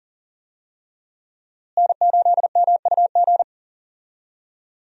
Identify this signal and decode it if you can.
Morse code, heard as a keyed tone, about 40 wpm, 700 Hz. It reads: N8MUG
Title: USA Call Signs - 5 Characters - Repeated 3x - 40wpm